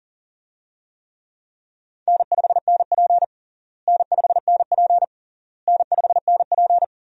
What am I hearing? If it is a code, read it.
Morse code, heard as a keyed tone, about 37 wpm, 700 Hz. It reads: N5NP N5NP N5NP
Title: USA Call Signs - 4 Characters - Repeated 3x - 40wpm